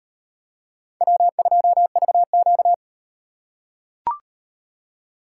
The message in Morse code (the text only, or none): W2VQ